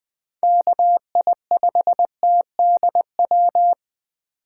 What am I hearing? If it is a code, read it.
Morse code, heard as a keyed tone, about 20 wpm, 700 Hz. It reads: KI5TDW
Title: 2023 POTA Activator Call Signs - 20wpm